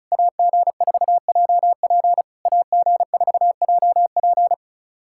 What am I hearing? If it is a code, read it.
Morse code, heard as a keyed tone, about 36 wpm, 690 Hz. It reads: AG4JP AG4JP
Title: SOTA QSO Practice 35wpm